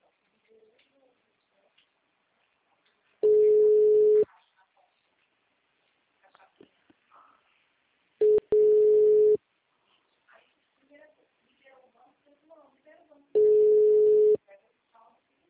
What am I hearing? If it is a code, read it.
Ringback tone (Europe, 425 Hz), with speech.